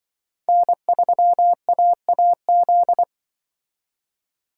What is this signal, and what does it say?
Morse code, heard as a keyed tone, about 24 wpm, 700 Hz. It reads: N3AAZ
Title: Speed Racing - USA 1x3 Call Signs 20wpm